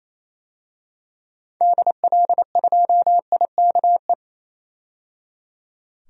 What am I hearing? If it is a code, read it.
Morse code, heard as a keyed tone, about 27 wpm, 700 Hz. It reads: DL2IKE
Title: DXCC-100 Call Signs - 6 Characters - 28wpm